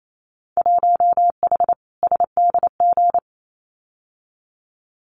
Morse code, heard as a keyed tone, about 28 wpm, 700 Hz. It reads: 1H SDG
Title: Winter Field Day Exchanges 28wpm